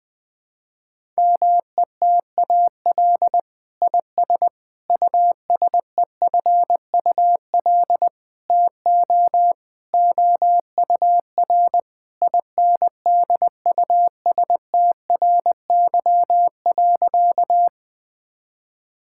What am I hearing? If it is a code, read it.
Morse code, heard as a keyed tone, about 20 wpm, 700 Hz. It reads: METAL IS USEFUL TO OUR INDUSTRY.
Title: Sentences from Top 1000 Words - Review Part 6 - 20wpm